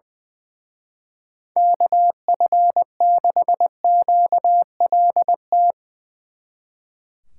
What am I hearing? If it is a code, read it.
Morse code, heard as a keyed tone, about 21 wpm, 700 Hz. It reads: KF6QLT
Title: Speed Racing - USA Call Signs 20wpm